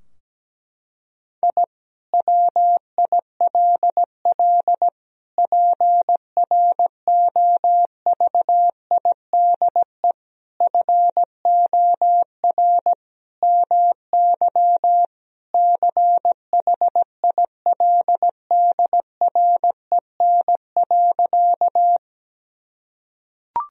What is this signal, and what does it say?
Morse code, heard as a keyed tone, about 17 wpm, 700 Hz. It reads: I WILL PROVIDE FOR MY CHILDREN.
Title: Sentences from Top 1000 Words - Review Part 3 - 17wpm